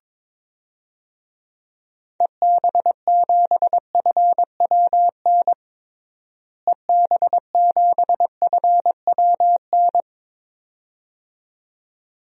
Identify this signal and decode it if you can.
Morse code, heard as a keyed tone, about 22 wpm, 700 Hz. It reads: EB7FWN EB7FWN
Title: DXCC-100 Call Signs - 6 Characters - Repeated 2x - 22wpm